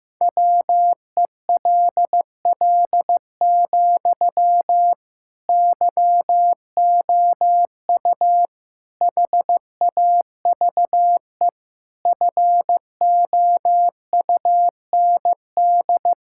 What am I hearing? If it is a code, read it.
Morse code, heard as a keyed tone, about 15 wpm, 700 Hz. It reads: WELL, YOU HAVE FOUND
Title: Sentences from Top 1000 Words - Review Part 6 - 15wpm